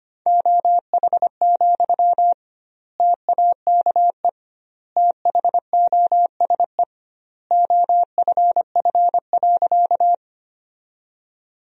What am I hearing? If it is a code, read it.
Morse code, heard as a keyed tone, about 25 wpm, 700 Hz. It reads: OH, TAKE THOSE OFF.
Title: 2x Word-Spacing - Sentences from Top 500 Words - 25wpm